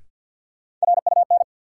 Morse code, heard as a keyed tone, about 51 wpm, 700 Hz. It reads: RUN
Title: Sentences from Top 300 Words - New Word Drill - 50wpm